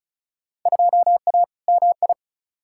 Morse code, heard as a keyed tone, about 36 wpm, 700 Hz. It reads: 2A MI
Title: Field Day Exchanges - 35wpm